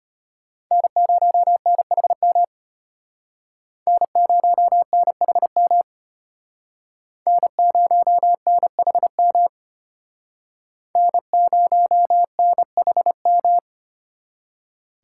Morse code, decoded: N0NHM N0NHM N0NHM N0NHM

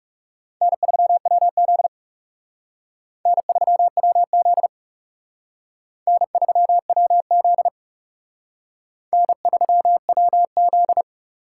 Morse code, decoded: N3WZ N3WZ N3WZ N3WZ

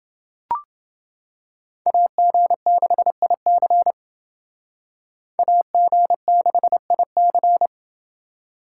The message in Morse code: AG6IC AG6IC

